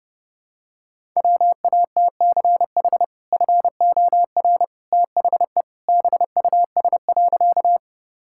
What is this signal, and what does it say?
Morse code, heard as a keyed tone, about 30 wpm, 700 Hz. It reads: WATCH FOR THE BUS.
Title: Sentences from Top 1000 Words - Review Part 6 - 30wpm